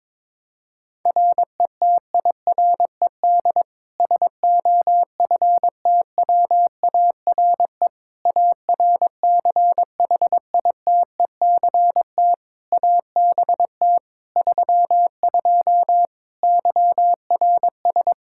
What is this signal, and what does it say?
Morse code, heard as a keyed tone, about 22 wpm, 700 Hz. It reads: RETIRED SOFTWARE ARCHITECT ABT 32 YRS
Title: Practice QSO - Exchange 3 - More Info - 22wpm